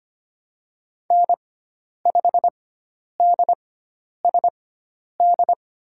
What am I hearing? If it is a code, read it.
Morse code, heard as a keyed tone, about 25 wpm, spaced as for 10 wpm, 700 Hz. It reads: N5DSD
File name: Lesson 16g - Letter c - Call Signs 25wpm